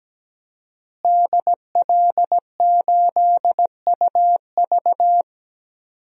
Morse code, heard as a keyed tone, about 17 wpm, 700 Hz. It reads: DL8UV
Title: DXCC-100 Call Signs - 5 Characters - Repeated 2x - 17wpm